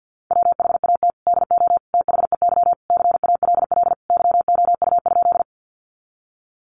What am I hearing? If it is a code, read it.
Morse code, heard as a keyed tone, about 50 wpm, 700 Hz. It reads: WHAT DO THEY CALL YOU?
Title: Sentences from Top 100 Words 50wpm